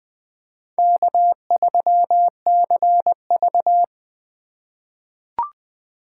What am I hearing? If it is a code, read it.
Morse code, heard as a keyed tone, about 20 wpm, 700 Hz. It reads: K3CV
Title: USA Call Signs - 4 Characters - 2x Character Spacing - 20wpm